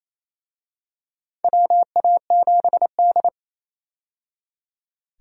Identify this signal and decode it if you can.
Morse code, heard as a keyed tone, about 27 wpm, 700 Hz. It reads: WA7D
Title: USA Call Signs - 4 Characters - 28wpm